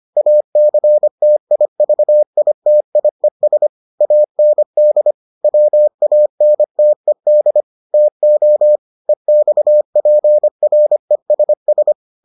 Morse code, heard as a keyed tone, about 25 wpm, 600 Hz. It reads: ACTIVITIES AND WANTED TO EXPRESS